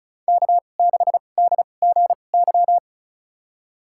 Morse code, heard as a keyed tone, about 36 wpm, spaced as for 25 wpm, 700 Hz. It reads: K6DGY
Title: USA Call Signs - 5 Characters - 2x Character Spacing - 35wpm